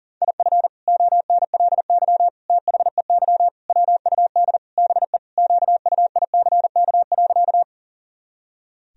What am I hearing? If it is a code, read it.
Morse code, heard as a keyed tone, about 40 wpm, 700 Hz. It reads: IF ONLY THEY WUD BE QUICK.
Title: Sentences from Top 500 Words and Abbreviations 40wpm